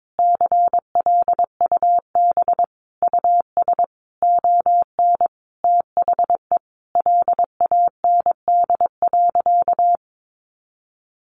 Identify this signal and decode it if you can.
Morse code, heard as a keyed tone, about 22 wpm, 700 Hz. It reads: CLUB US ON THE LAND.
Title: Sentences from Top 1200 Words 22wpm